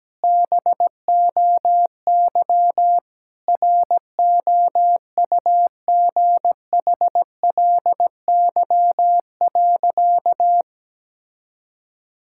Morse code, heard as a keyed tone, about 17 wpm, 700 Hz. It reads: BOY ROUGHLY.
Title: Sentences from Top 1600 Words 17wpm